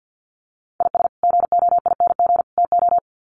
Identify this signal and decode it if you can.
Morse code, heard as a keyed tone, about 50 wpm, 700 Hz. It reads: IS GOING TO
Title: Sets of 3 Words 50wpm